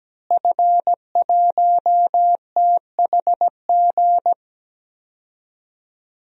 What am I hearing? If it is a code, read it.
Morse code, heard as a keyed tone, about 17 wpm, 700 Hz. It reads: F1THG
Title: DXCC-100 Call Signs - 5 Characters - Repeated 3x - 17wpm